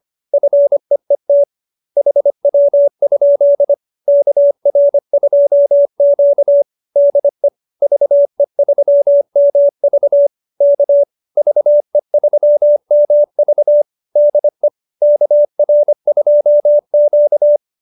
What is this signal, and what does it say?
Morse code, heard as a keyed tone, about 25 wpm, 570 Hz. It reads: FEET HW? KR2Q DE VE3MV K VE3MV DE KR2Q